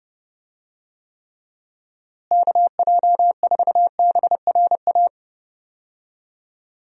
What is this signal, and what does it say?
Morse code, heard as a keyed tone, about 30 wpm, 700 Hz. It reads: KJ4BRA
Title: USA Call Signs - 6 Characters - Repeated 3x - 30wpm